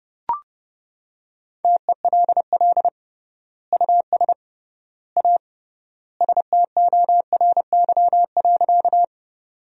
Morse code, heard as a keyed tone, about 32 wpm, 700 Hz. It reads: TELL US A STORY.
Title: 3x Word-Spacing - Sentences from Top 300 Words - 30wpm